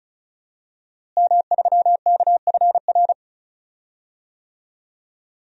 Morse code, heard as a keyed tone, about 34 wpm, 700 Hz. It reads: M3KFR
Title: DXCC-100 Call Signs - 5 Characters - Repeated 2x - 35wpm